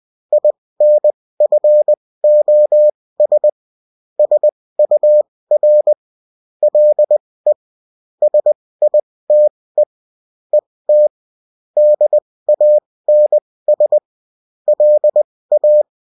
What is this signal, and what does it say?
Morse code, heard as a keyed tone, about 20 wpm, spaced as for 16 wpm, 600 Hz. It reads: INFOS SUR LE SITE ET DANS LA
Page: bulletin-uft-sous-forme-texte-et-en-cw